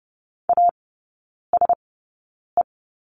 Morse code, heard as a keyed tone, about 30 wpm, spaced as for 9 wpm, 700 Hz. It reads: ASE
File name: course / 30:10wpm / 087 Lesson 15e - Number 5 - Three Characters 30wpm